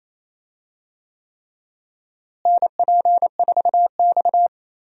Morse code, heard as a keyed tone, about 28 wpm, 700 Hz. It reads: NP4X
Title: USA Call Signs - 4 Characters - Repeated 3x - 28wpm